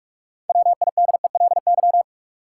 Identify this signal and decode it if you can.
Morse code, heard as a keyed tone, about 46 wpm, 700 Hz. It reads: WIDELY
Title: Sentences from Top 1200 Words - New Word Drill Alt - 45wpm